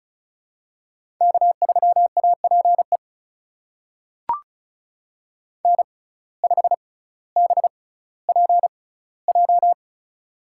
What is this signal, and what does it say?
Morse code, heard as a keyed tone, about 36 wpm, spaced as for 18 wpm, 700 Hz. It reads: K3APE N5BPJ
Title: USA Call Signs - 5 Characters - 6x Character Spacing - 35wpm